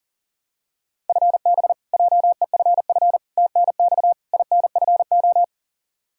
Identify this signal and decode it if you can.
Morse code, heard as a keyed tone, about 40 wpm, 700 Hz. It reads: FB JEFF TNX INFO
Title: Practice QSO - Exchange 3 - More Info - 40wpm